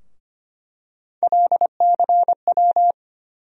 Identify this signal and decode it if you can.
Morse code, heard as a keyed tone, about 25 wpm, 700 Hz. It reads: LCW
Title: USA Call Sign Postfixes - 3 Letters 25wpm